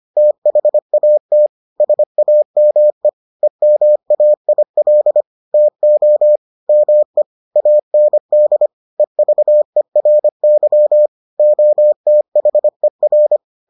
Morse code, heard as a keyed tone, about 25 wpm, 600 Hz. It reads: THAT SAME EMAIL TO ME AND EVERY OTHER